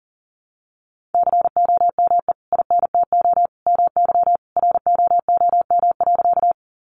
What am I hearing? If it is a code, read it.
Morse code, heard as a keyed tone, about 40 wpm, 700 Hz. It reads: COME INTO MY ROOM.
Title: Sentences from Top 600 Words 40wpm